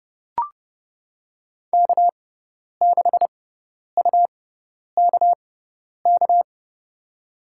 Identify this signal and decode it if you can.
Morse code, heard as a keyed tone, about 30 wpm, spaced as for 10 wpm, 700 Hz. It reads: K6UKK
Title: USA Call Signs - 6x Character Spacing - 30wpm